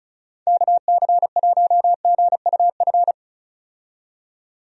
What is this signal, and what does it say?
Morse code, heard as a keyed tone, about 34 wpm, 700 Hz. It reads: KC1GUF
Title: USA Call Signs - 6 Characters - 35wpm